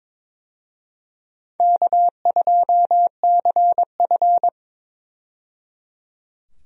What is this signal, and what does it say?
Morse code, heard as a keyed tone, about 22 wpm, 700 Hz. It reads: K2CF
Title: USA Call Signs - 4 Characters - 22wpm